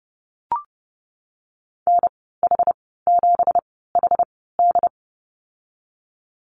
Morse code, heard as a keyed tone, about 30 wpm, spaced as for 17 wpm, 700 Hz. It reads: NH7HD
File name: Lesson 37g - Letter x - Call Signs 30wpm